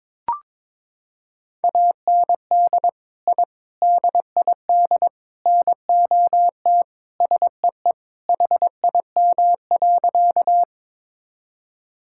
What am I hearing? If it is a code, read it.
Morse code, heard as a keyed tone, about 22 wpm, 700 Hz. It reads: AND I DID NOT SEE HIM.
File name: practice / 22wpm / Sentences with 4-Word Phrases - 22wpm